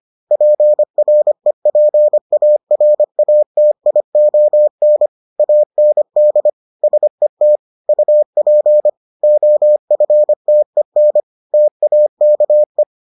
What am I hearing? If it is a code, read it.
Morse code, heard as a keyed tone, about 25 wpm, 600 Hz. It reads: PREPARATION AND SET UP OFTEN TAKE